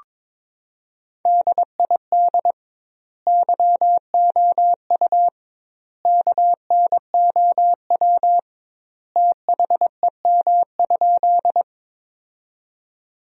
Morse code, heard as a keyed tone, about 22 wpm, 700 Hz. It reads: DID YOU KNOW THEM?
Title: 2x Word-Spacing - Sentences from Top 100 Words - 22wpm